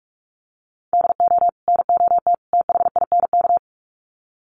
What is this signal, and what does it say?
Morse code, heard as a keyed tone, about 44 wpm, 700 Hz. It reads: DO NOT THINK
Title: Sets of 3 Words 45wpm